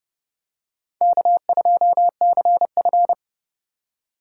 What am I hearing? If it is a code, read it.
Morse code, heard as a keyed tone, about 30 wpm, 700 Hz. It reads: K2CF
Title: USA Call Signs - 4 Characters - 4x Character Spacing - 30wpm